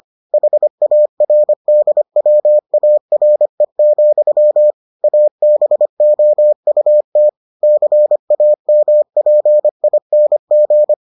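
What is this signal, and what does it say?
Morse code, heard as a keyed tone, about 25 wpm, 600 Hz. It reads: HARDWARE, ABOUT CAMPING
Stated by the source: Morse edition of the Foundations of Amateur Radio episodes.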